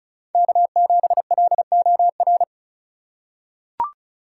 Morse code, heard as a keyed tone, about 36 wpm, 700 Hz. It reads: K7LOR
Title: USA Call Signs - 5 Characters - 2x Character Spacing - 35wpm